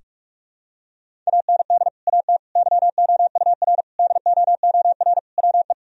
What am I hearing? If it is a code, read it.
Morse code, heard as a keyed tone, about 46 wpm, 700 Hz. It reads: AND AT YOUR DOOR WE
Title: Sentences from Top 1000 Words - Review Part 1 - 45wpm